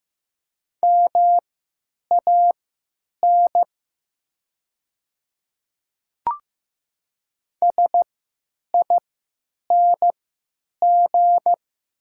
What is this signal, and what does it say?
Morse code, heard as a keyed tone, about 15 wpm, spaced as for 9 wpm, 700 Hz. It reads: MAN SING
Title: 3x Character Spacing - Top 500 Words - 15wpm